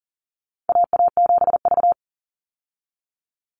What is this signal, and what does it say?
Morse code, heard as a keyed tone, about 40 wpm, 700 Hz. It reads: AA7V
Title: Call Signs Repeated 2x - 40wpm